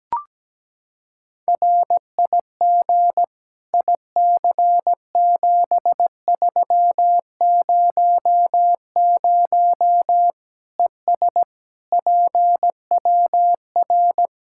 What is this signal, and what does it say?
Morse code, heard as a keyed tone, about 17 wpm, 700 Hz. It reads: RIG IC7300 ES PWR
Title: Practice QSO - Exchange 2 - Info - 17wpm